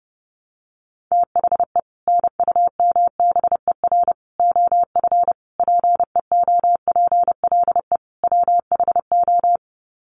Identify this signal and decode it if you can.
Morse code, heard as a keyed tone, about 30 wpm, 700 Hz. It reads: THE NUMBER OF PEOPLE WHO
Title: Sets of 5 Words Encore 30wpm